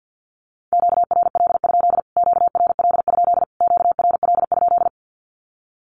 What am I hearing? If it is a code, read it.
Morse code, heard as a keyed tone, about 50 wpm, 700 Hz. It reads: QRL? QRL? QRL?
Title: Q-Codes Repeated 3x 50wpm